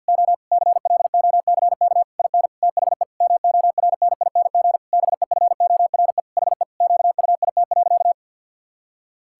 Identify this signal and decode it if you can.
Morse code, heard as a keyed tone, about 52 wpm, 700 Hz. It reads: O CLOCK IN THE MORNING BEFORE HE QUIT.
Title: Pangram Challenge v2 - 50wpm